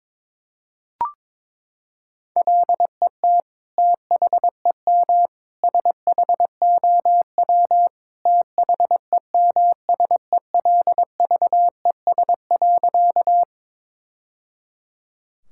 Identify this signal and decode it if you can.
Morse code, heard as a keyed tone, about 22 wpm, 700 Hz. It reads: LET THEM SHOW THEMSELVES.
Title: Sentences from Top 1000 Words - Review Part 5 - 22wpm